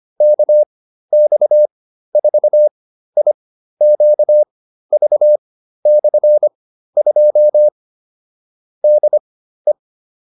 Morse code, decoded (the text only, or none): KX4IQV/2 DE